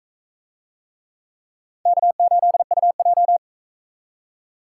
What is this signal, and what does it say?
Morse code, heard as a keyed tone, about 39 wpm, 700 Hz. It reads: K8UJ